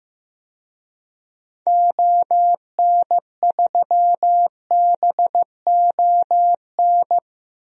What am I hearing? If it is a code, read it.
Morse code, heard as a keyed tone, about 16 wpm, 700 Hz. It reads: ON3BON
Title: DXCC-100 Call Signs - 6 Characters - Repeated 2x - 15wpm